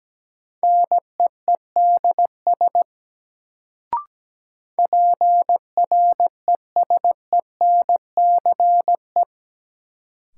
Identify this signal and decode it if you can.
Morse code, heard as a keyed tone, about 17 wpm, 700 Hz. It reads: NEEDS PRESENCE
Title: Sets of 3 Words - New Word Drill - 17wpm